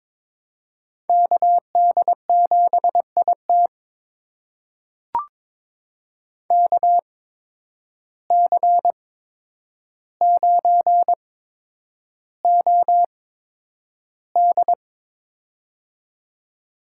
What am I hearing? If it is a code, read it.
Morse code, heard as a keyed tone, about 22 wpm, spaced as for 9 wpm, 700 Hz. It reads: KD7IT KC9OD
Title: USA Call Signs - 5 Characters - 8x Character Spacing - 22wpm